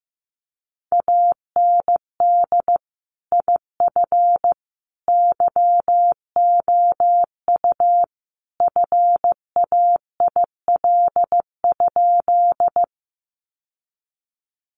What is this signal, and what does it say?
Morse code, heard as a keyed tone, about 15 wpm, 700 Hz. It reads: AND IF YOU FAIL?
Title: Sentences from Top 1100 Words 15wpm